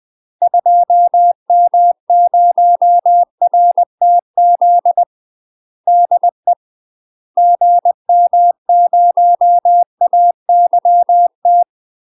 Morse code, heard as a keyed tone, about 20 wpm, 700 Hz. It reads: 2M0RTZ DE GM0AYT
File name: UK MASTER QSOs TRAINING / 20WPM / UK MASTER QSOs 0040